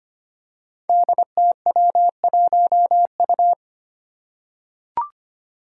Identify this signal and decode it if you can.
Morse code, heard as a keyed tone, about 25 wpm, 700 Hz. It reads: DTW1U
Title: Five Challenging Letter-Number Combinations 25wpm